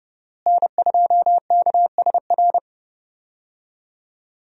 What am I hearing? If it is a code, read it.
Morse code, heard as a keyed tone, about 31 wpm, 700 Hz. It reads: N2KSR
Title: USA Call Signs - 5 Characters - Repeated 3x - 30wpm